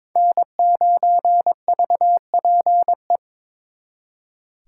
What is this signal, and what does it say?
Morse code, heard as a keyed tone, about 26 wpm, 700 Hz. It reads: N9VPE
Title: Speed Racing - USA 1x3 Call Signs 22wpm